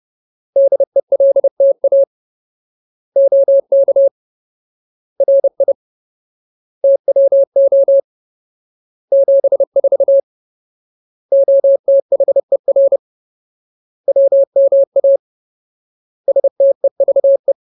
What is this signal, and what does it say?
Morse code, heard as a keyed tone, about 30 wpm, 550 Hz. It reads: DELTA OK RI TWO 74 OTHER WMA STEVE
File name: Ultimate Word List / 30WPM Ultimate